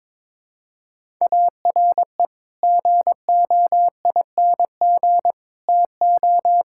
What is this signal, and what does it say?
Morse code, heard as a keyed tone, about 22 wpm, 700 Hz. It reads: ARE GOING TO
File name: practice / 22wpm / Build A Phrase - 5-Word Phrases - 22wpm